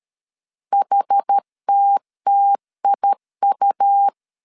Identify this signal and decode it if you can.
Morse code, heard as a keyed tone, about 13 wpm, 800 Hz. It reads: HTTIU
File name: CW / DET / 1 3 1500C S B 12wpm 2828